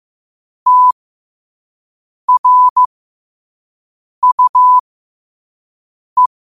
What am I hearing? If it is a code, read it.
Morse code, heard as a keyed tone, about 15 wpm, spaced as for 5 wpm, 1000 Hz. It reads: TRUE
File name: qotdfiles / 5wpm